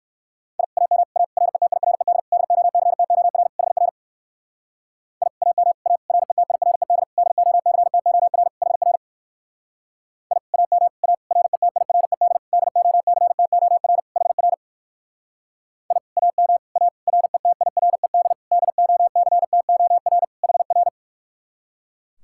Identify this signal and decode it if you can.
Morse code, heard as a keyed tone, about 55 wpm, 700 Hz. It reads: I AM A RETIRED DOCTOR HR I AM A RETIRED DOCTOR HR I AM A RETIRED DOCTOR HR I AM A RETIRED DOCTOR HR
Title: Speed-Racing Common QSO Phrases - 45wpm